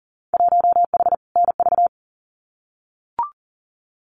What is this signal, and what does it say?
Morse code, heard as a keyed tone, about 40 wpm, 700 Hz. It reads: 1H NV
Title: Winter Field Day Exchanges 40wpm